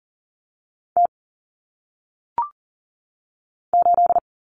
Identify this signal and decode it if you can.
Morse code, heard as a keyed tone, about 39 wpm, 700 Hz. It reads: T 8
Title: Common Cut Numbers - 1-Digit 40wpm